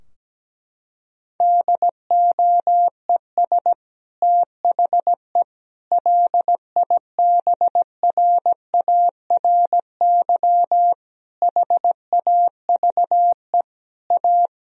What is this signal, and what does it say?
Morse code, heard as a keyed tone, about 17 wpm, 700 Hz. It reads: DOES THE LIBRARY HAVE A
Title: Sentences from Top 1000 Words - Review Part 4 - 17wpm